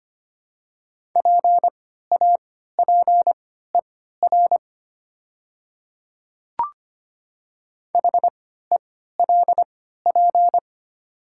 Morse code, heard as a keyed tone, about 25 wpm, spaced as for 14 wpm, 700 Hz. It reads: PAPER HELP
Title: 3x Character Spacing - Top 500 Words - 25wpm